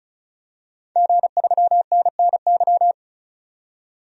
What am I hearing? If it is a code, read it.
Morse code, heard as a keyed tone, about 34 wpm, 700 Hz. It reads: G3NNY
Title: DXCC-100 Call Signs - 5 Characters - 35wpm